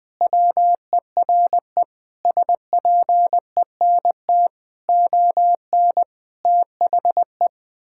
Morse code, heard as a keyed tone, about 20 wpm, 700 Hz. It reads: WERE SPENT ON THE
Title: Sentences with 5-Word Phrases - 20wpm